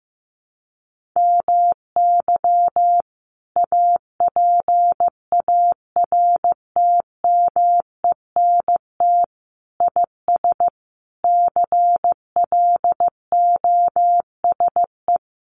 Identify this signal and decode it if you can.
Morse code, heard as a keyed tone, about 15 wpm, 700 Hz. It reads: MY APARTMENT IS CLOSE
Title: Sentences from Top 900 Words 15wpm